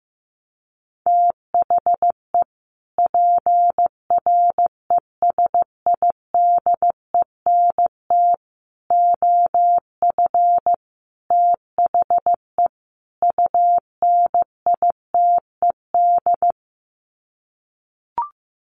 Morse code, heard as a keyed tone, about 15 wpm, 700 Hz. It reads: THE PRESIDENT OF THE UNITED
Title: Sets of 5 Words 15wpm